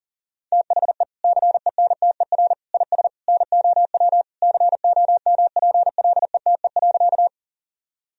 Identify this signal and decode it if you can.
Morse code, heard as a keyed tone, about 40 wpm, 700 Hz. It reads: THE CENTER IS NOW COMPLETE.
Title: Sentences from Top 1000 Words - Review Part 2 - 40wpm